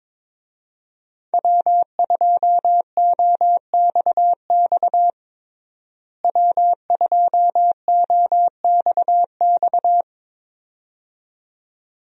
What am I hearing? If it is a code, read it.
Morse code, heard as a keyed tone, about 22 wpm, 700 Hz. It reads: W2OXX W2OXX
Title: USA Call Signs - 5 Characters - Repeated 2x - 22wpm